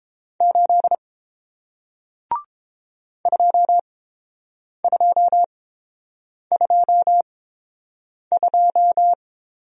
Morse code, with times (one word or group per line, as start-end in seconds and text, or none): 0.4-0.9 8
3.3-9.1 2222